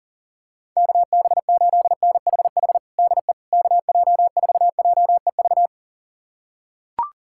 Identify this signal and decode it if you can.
Morse code, heard as a keyed tone, about 40 wpm, 700 Hz. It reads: KB8NHH DE KJ4JEV
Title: USA Callsign DE Callsign 40wpm